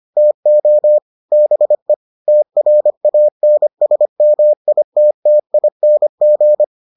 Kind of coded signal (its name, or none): Morse code